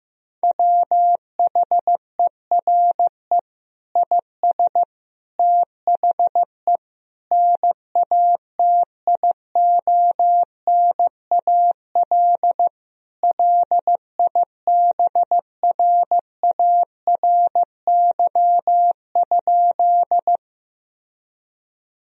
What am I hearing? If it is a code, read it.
Morse code, heard as a keyed tone, about 15 wpm, 700 Hz. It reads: WHERE IS THE NATIONAL LIBRARY?
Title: Sentences from Top 1000 Words - Review Part 4 - 15wpm